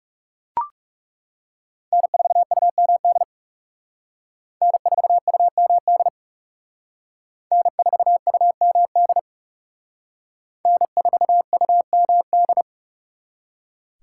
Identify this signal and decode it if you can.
Morse code, heard as a keyed tone, about 36 wpm, 700 Hz. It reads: N4UMD N4UMD N4UMD N4UMD
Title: Speed Racing - USA Call Signs 30wpm